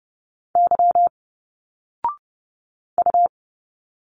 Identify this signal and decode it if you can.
Morse code, heard as a keyed tone, about 30 wpm, 700 Hz. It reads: Y U
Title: Single Letters 30wpm